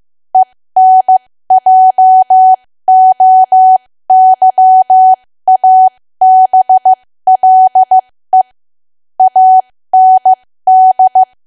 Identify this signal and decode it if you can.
Morse code, heard as a keyed tone, about 15 wpm, spaced as for 13 wpm, 750 Hz. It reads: ENJOYABLE AND
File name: cwa / W1AW-Level2 / 13WPM